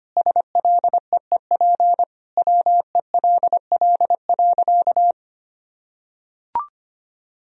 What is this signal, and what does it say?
Morse code, heard as a keyed tone, about 25 wpm, 700 Hz. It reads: SLEEP WELL.
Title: Sentences from Top 500 Words and Abbreviations 25wpm